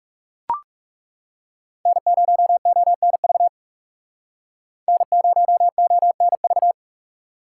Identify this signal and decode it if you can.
Morse code, heard as a keyed tone, about 37 wpm, 700 Hz. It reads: N0ONV N0ONV
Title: Speed Racing - USA 1x3 Call Signs 30wpm